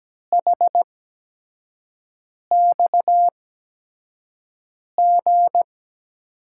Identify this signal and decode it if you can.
Morse code, heard as a keyed tone, about 17 wpm, spaced as for 5 wpm, 700 Hz. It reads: HXG
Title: USA Call Signs - 8x Character Spacing - 17wpm